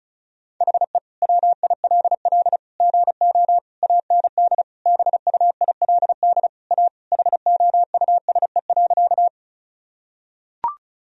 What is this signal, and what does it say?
Morse code, heard as a keyed tone, about 34 wpm, 700 Hz. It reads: HE WILL GO AND BUILD A HOUSE.
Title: Sentences from Top 1000 Words - Review Part 1 - 35wpm